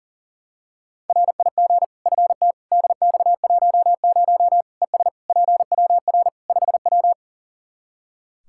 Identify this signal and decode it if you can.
Morse code, heard as a keyed tone, about 37 wpm, 700 Hz. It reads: RIG FT DX10 ES PWR 5W
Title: Practice QSO - Exchange 2 - Info - 40wpm